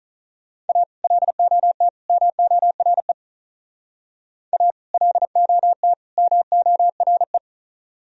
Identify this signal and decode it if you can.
Morse code, heard as a keyed tone, about 42 wpm, 700 Hz. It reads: A LOT MORE A LOT MORE
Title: Speed-Racing - Sets of 3 Words 35wpm